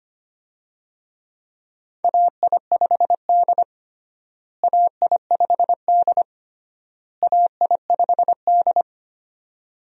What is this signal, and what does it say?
Morse code, heard as a keyed tone, about 25 wpm, 700 Hz. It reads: AI5D AI5D AI5D
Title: USA Call Signs - 4 Characters - Repeated 3x - 25wpm